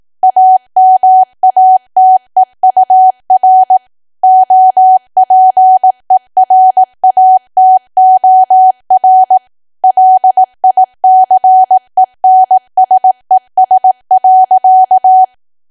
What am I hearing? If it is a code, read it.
Morse code, heard as a keyed tone, about 18 wpm, 750 Hz. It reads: AMATEUR OPERATOR LICENSES.
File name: cwa / W1AW-Level2 / 18WPM